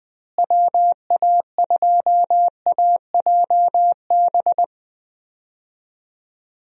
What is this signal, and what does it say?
Morse code, heard as a keyed tone, about 20 wpm, 700 Hz. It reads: WA2AJB
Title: USA Call Signs - 6 Characters - Repeated 3x - 20wpm